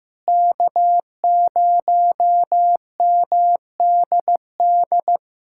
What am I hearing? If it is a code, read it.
Morse code, heard as a keyed tone, about 15 wpm, 700 Hz. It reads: K0MDD
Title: USA Callsign DE Callsign 15wpm